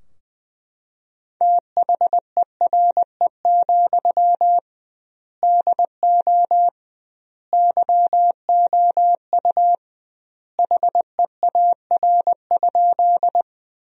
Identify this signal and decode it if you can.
Morse code, heard as a keyed tone, about 20 wpm, 700 Hz. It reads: THERE, DO YOU HEAR?
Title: 2x Word-Spacing - Sentences from Top 300 Words - 20wpm